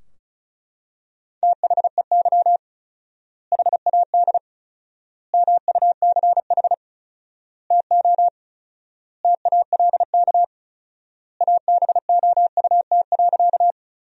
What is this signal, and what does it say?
Morse code, heard as a keyed tone, about 36 wpm, 700 Hz. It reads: THEY HAD MUCH TO TALK ABOUT.
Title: 4x Word-Spacing - Sentences from Top 400 Words - 35wpm